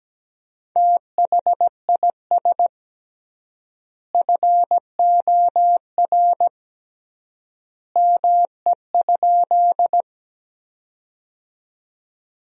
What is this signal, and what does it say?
Morse code, heard as a keyed tone, about 17 wpm, 700 Hz. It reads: THIS FOR ME?
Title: 3x Word-Spacing - Sentences from Top 200 Words - 17wpm